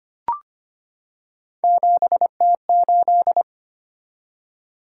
Morse code, heard as a keyed tone, about 25 wpm, 700 Hz. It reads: 7T8
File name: Three Letter-Number Combinations - 25wpm